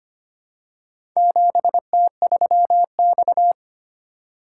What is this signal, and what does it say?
Morse code, heard as a keyed tone, about 25 wpm, 700 Hz. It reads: 7T3X
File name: practice / 25wpm / Four Letter-Number Combinations - 25wpm